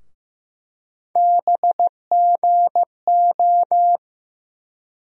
Morse code, heard as a keyed tone, about 15 wpm, 700 Hz. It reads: BGO
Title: USA Call Sign Postfixes - 3 Letters 15wpm